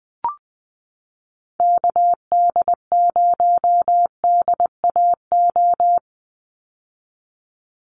Morse code, heard as a keyed tone, about 20 wpm, 700 Hz. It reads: KD0DAO
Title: USA Call Signs 20wpm